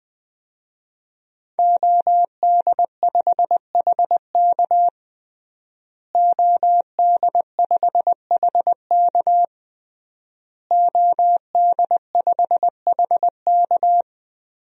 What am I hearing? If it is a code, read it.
Morse code, heard as a keyed tone, about 20 wpm, 700 Hz. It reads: OD5HK OD5HK OD5HK
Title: DXCC-100 Call Signs - 5 Characters - Repeated 3x - 20wpm